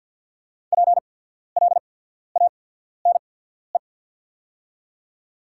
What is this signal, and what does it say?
Morse code, heard as a keyed tone, about 48 wpm, spaced as for 14 wpm, 690 Hz. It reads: PLANE
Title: 8x Character Spacing - Top 500 Words - 50wpm